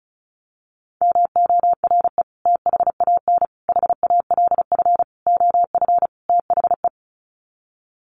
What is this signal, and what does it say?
Morse code, heard as a keyed tone, about 34 wpm, 700 Hz. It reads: MORE THAN HALF OF THE
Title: Sets of 5 Words 35wpm